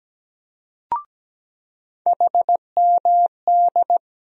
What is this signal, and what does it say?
Morse code, heard as a keyed tone, about 17 wpm, 700 Hz. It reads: HMD
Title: USA Call Sign Postfixes - 3 Letters 17wpm